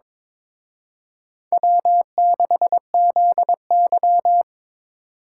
Morse code, heard as a keyed tone, about 25 wpm, 700 Hz. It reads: W6ZY
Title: Speed Racing - USA 1x2 Call Signs 22wpm